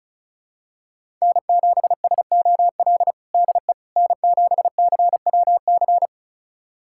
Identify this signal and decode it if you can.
Morse code, heard as a keyed tone, about 36 wpm, 700 Hz. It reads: N7SOL DE N7CWC
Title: USA Callsign DE Callsign - 5 Characters - 35wpm